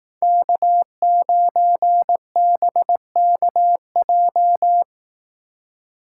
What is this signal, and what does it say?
Morse code, heard as a keyed tone, about 18 wpm, 700 Hz. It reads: K9BKJ
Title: Speed Racing - USA 1x3 Call Signs 15wpm